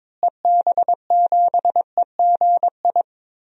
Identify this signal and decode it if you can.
Morse code, heard as a keyed tone, about 22 wpm, 700 Hz. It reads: EB7EGI
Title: DXCC-100 Call Signs - 6 Characters - Repeated 3x - 22wpm